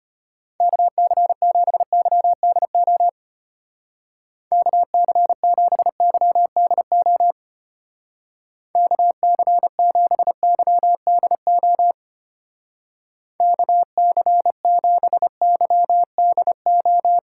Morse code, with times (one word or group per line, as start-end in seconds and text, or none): 0.6-3.1 KC7YDO
4.5-7.3 KC7YDO
8.8-11.9 KC7YDO
13.4-17.2 KC7YDO